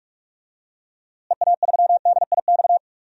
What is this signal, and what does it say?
Morse code, heard as a keyed tone, about 44 wpm, 700 Hz. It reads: EA3DIX
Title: DXCC-100 Call Signs - 6 Characters - Repeated 3x - 45wpm